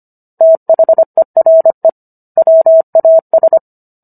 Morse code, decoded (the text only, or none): THERE WAS